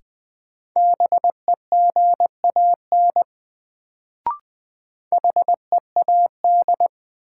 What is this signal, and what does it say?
Morse code, heard as a keyed tone, about 20 wpm, 700 Hz. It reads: BEGAN HEAD
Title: Sentences from Top 300 Words - New Word Drill - 20wpm